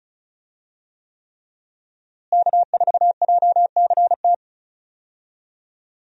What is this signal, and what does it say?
Morse code, heard as a keyed tone, about 36 wpm, 700 Hz. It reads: K4JCT
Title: USA Call Signs - 5 Characters - Repeated 3x - 35wpm